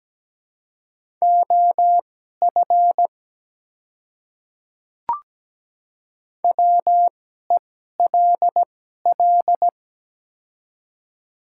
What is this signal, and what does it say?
Morse code, heard as a keyed tone, about 17 wpm, spaced as for 12 wpm, 690 Hz. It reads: OF WELL
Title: 2x Character Spacing - Top 300 Words - 17wpm